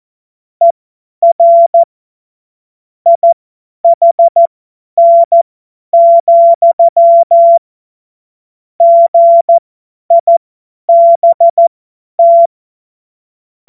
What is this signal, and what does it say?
Morse code, heard as a keyed tone, about 14 wpm, spaced as for 10 wpm, 680 Hz. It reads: ER IHN, GIBT
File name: Morsekurs / Witze / Witz 04. Oktober 2022 50 F70